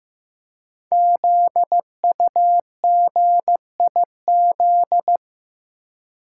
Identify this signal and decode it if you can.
Morse code, heard as a keyed tone, about 15 wpm, 700 Hz. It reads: ZUGIZ
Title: Code Groups - 5 Letter:Numbers - 15wpm